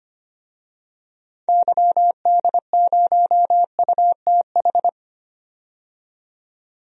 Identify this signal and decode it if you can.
Morse code, heard as a keyed tone, about 25 wpm, 700 Hz. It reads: YD0UTH
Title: DXCC-100 Call Signs - 6 Characters - 25wpm